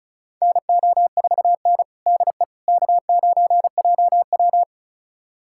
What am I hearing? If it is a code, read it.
Morse code, heard as a keyed tone, about 36 wpm, 690 Hz. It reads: NO4N DE K9JW
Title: USA Callsign DE Callsign - 4 Characters - 35wpm